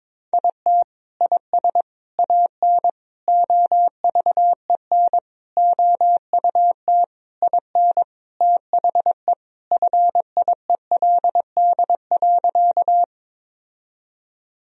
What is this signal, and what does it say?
Morse code, heard as a keyed tone, about 22 wpm, 700 Hz. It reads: IT IS AN OVEN OUT IN THE FIELD.
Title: Sentences from Top 1000 Words - Review Part 4 - 22wpm